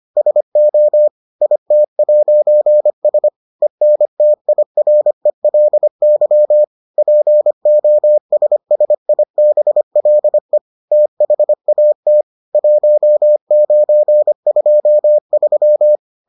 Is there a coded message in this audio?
Yes: Morse code from 0.2 to 16.0 s, heard as a keyed tone, about 25 wpm, 600 Hz. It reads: SO IT'S ENTIRELY POSSIBLE THAT 1923